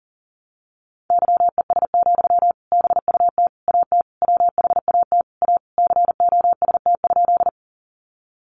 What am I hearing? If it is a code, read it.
Morse code, heard as a keyed tone, about 40 wpm, 700 Hz. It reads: YES, BUT AT WHAT A COST?
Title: Sentences from Top 800 Words 40wpm